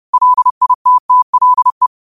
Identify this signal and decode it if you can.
Morse code, heard as a keyed tone, about 30 wpm, 1000 Hz. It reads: LITTLE
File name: qotdfiles / 30wpm